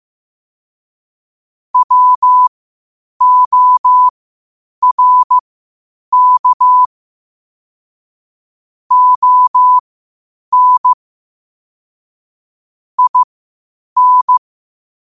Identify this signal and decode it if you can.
Morse code, heard as a keyed tone, about 15 wpm, spaced as for 8 wpm, 1000 Hz. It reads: WORK ON IN
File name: qotdfiles / 8wpm